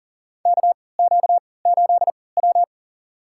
Morse code, heard as a keyed tone, about 40 wpm, spaced as for 23 wpm, 700 Hz. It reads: KQ8W
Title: USA Call Signs - 4 Characters - 3x Character Spacing - 40wpm